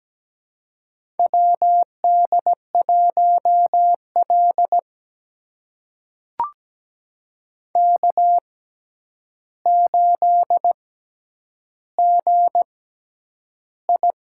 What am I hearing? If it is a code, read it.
Morse code, heard as a keyed tone, about 17 wpm, spaced as for 9 wpm, 700 Hz. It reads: WD1L K8GI
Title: USA Call Signs - 4 Characters - 6x Character Spacing - 17wpm